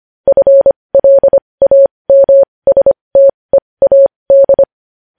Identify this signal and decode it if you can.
Morse code, heard as a keyed tone, about 25 wpm, spaced as for 20 wpm, 550 Hz. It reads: FLAMSTEAD
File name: podcast / News headlines-1x-25x20